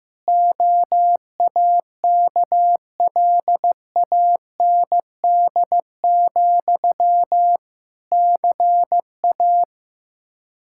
Morse code, heard as a keyed tone, about 16 wpm, 700 Hz. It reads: OAKLAND, CA
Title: K1USN SST QSO Practice 15wpm